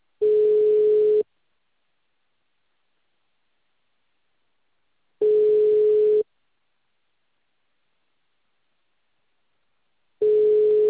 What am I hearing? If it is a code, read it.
Ringback tone (Europe, 425 Hz).